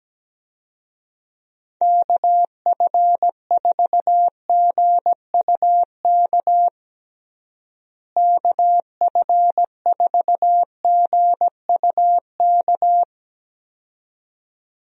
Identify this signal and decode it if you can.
Morse code, heard as a keyed tone, about 17 wpm, 700 Hz. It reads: KF4GUK KF4GUK
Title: USA Call Signs - 6 Characters - Repeated 2x - 17wpm